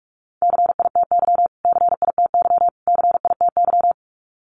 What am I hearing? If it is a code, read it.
Morse code, heard as a keyed tone, about 44 wpm, 700 Hz. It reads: CITY CITY CITY
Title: Top 500 Words Repeated 45wpm